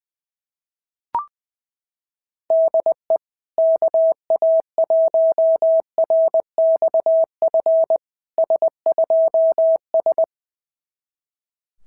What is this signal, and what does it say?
Morse code, heard as a keyed tone, about 20 wpm, 690 Hz. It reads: DE KA1RXF S2S
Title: SOTA QSO Practice 20wpm